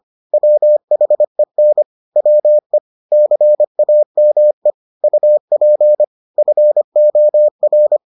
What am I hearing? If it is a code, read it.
Morse code, heard as a keyed tone, about 25 wpm, 600 Hz. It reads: WHEN WE CAME UP FOR